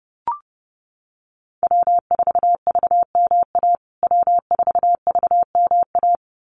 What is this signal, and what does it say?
Morse code, heard as a keyed tone, about 29 wpm, 700 Hz. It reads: W4VMA W4VMA
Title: USA Call Signs Repeated 2x 30wpm